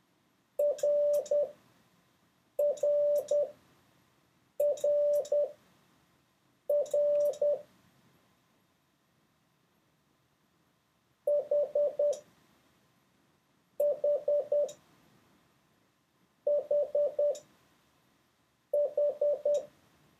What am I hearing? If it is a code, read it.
Morse code, heard as a keyed tone, about 10 wpm, spaced as for 5 wpm, 580 Hz. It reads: RRRR HH HH
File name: 3of6-mix-and-match